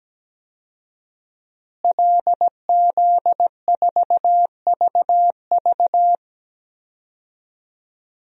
Morse code, heard as a keyed tone, about 17 wpm, 700 Hz. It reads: LZ4VV
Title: DXCC-100 Call Signs - 5 Characters - Repeated 2x - 17wpm